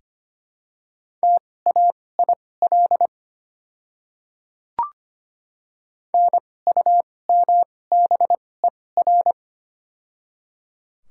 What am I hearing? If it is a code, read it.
Morse code, heard as a keyed tone, about 25 wpm, spaced as for 18 wpm, 700 Hz. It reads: TAIL NUMBER
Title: 2x Character Spacing - Top 500 Words - 25wpm